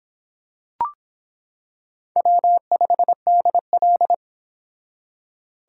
Morse code, heard as a keyed tone, about 26 wpm, 700 Hz. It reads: W5DL